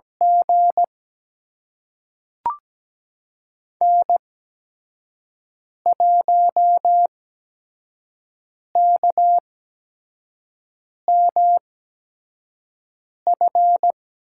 Morse code, heard as a keyed tone, about 17 wpm, spaced as for 5 wpm, 700 Hz. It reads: G N1KMF
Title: USA Call Signs - 8x Character Spacing - 17wpm